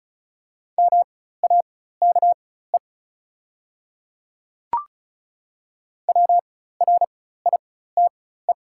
Morse code, heard as a keyed tone, about 34 wpm, spaced as for 16 wpm, 700 Hz. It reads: MAKE WRITE
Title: 4x Character Spacing - Top 100 Words - 35wpm